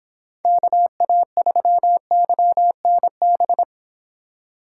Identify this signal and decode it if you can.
Morse code, heard as a keyed tone, about 26 wpm, 700 Hz. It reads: KA3YNB